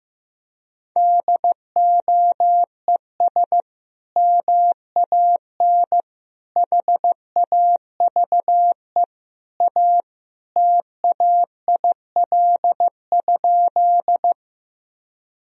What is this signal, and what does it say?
Morse code, heard as a keyed tone, about 15 wpm, 700 Hz. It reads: DOES MAN HAVE A TAIL?
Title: Sentences from Top 1000 Words - Review Part 3 - 15wpm